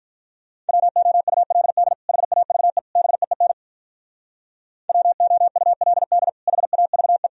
Morse code, heard as a keyed tone, about 44 wpm, 700 Hz. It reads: WOULD HAVE BEEN WOULD HAVE
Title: Speed-Racing - Sets of 3 Words 35wpm